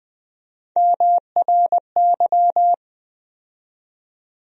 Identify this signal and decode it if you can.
Morse code, heard as a keyed tone, about 20 wpm, 700 Hz. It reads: MRY
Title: USA Call Sign Postfixes - 3 Letters 20wpm